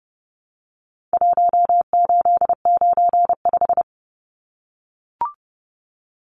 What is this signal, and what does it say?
Morse code, heard as a keyed tone, about 30 wpm, 700 Hz. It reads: 1895
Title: Numbers - Contest Style - 4-Digit 30wpm